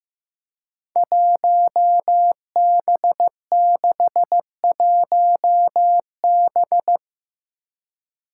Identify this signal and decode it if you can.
Morse code, heard as a keyed tone, about 15 wpm, 700 Hz. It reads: 1B61B
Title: Five Challenging Letter-Number Combinations 15wpm